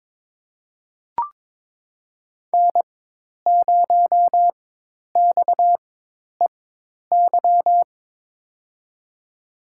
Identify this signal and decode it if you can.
Morse code, heard as a keyed tone, about 22 wpm, spaced as for 10 wpm, 700 Hz. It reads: N0XEY
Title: USA Call Signs - 4x Character Spacing - 22wpm